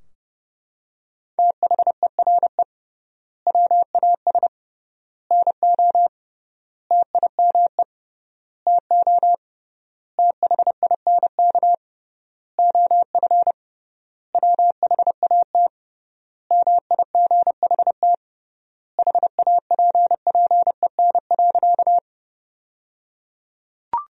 Morse code, heard as a keyed tone, about 30 wpm, 700 Hz. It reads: THERE WAS NO TIME TO THINK OF WHAT MIGHT HAPPEN.
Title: 3x Word-Spacing - Sentences from Top 400 Words - 30wpm